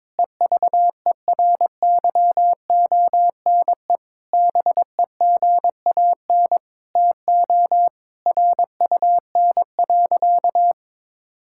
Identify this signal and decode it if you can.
Morse code, heard as a keyed tone, about 21 wpm, 700 Hz. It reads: EVERYONE BEGAN TO RUN.
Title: Sentences from Top 1600 Words 22wpm